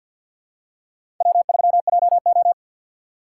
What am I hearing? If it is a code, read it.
Morse code, heard as a keyed tone, about 47 wpm, 700 Hz. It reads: W3JO
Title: USA Call Signs - 4 Characters - 6x Character Spacing - 50wpm